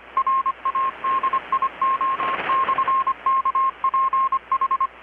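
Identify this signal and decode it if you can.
Morse code, heard as a keyed tone, about 25 wpm, 1070 Hz, with radio noise. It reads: RADIO/KPH
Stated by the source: Morse from KPH's Night of Nights XX maritime radiotelegraphy event, received in Chiapas, Mexico about 3,800 km away on a Kenwood R-600 with a grounded 10 m long-wire antenna.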